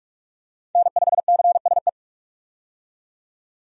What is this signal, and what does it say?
Morse code, heard as a keyed tone, about 41 wpm, 700 Hz. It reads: N5CSE
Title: USA Call Signs - 5 Characters - 45wpm